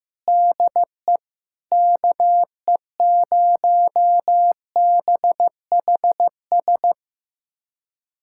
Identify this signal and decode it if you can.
Morse code, heard as a keyed tone, about 15 wpm, 700 Hz. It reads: DE KE0BHS
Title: USA Callsign DE Callsign - 6 Characters - 15wpm